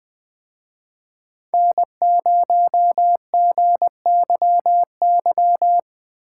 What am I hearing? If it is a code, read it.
Morse code, heard as a keyed tone, about 20 wpm, 700 Hz. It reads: N0GYY
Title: Speed Racing - USA 1x3 Call Signs 17wpm